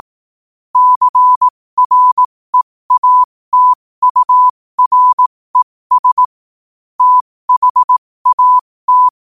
Morse code, heard as a keyed tone, about 18 wpm, spaced as for 15 wpm, 1000 Hz. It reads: CREATURES THAT
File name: qotdfiles / 15wpm